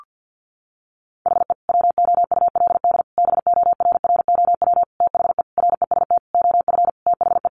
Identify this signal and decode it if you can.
Morse code, heard as a keyed tone, about 50 wpm, 700 Hz. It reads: HE WOULD BORROW THE REST OF THE
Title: Sentences from Top 1500 Words 50wpm